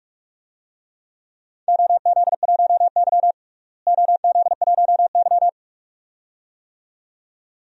Morse code, heard as a keyed tone, about 46 wpm, 700 Hz. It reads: OZ1Y OZ1Y
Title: DXCC-100 Call Signs - 4 Characters - Repeated 2x - 45wpm